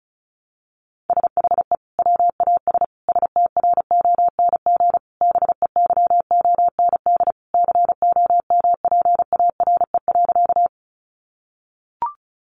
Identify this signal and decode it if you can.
Morse code, heard as a keyed tone, about 34 wpm, 700 Hz. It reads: SHE WAS STRONG BEYOND COMPARE.
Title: Sentences from Top 1300 Words 35wpm